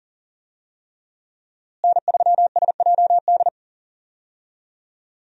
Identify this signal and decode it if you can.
Morse code, heard as a keyed tone, about 40 wpm, 700 Hz. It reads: N3SJD
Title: USA Call Signs - 5 Characters - Repeated 3x - 40wpm